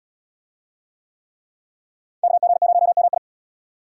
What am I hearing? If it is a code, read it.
Morse code, heard as a keyed tone, about 66 wpm, 700 Hz. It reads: KG0MI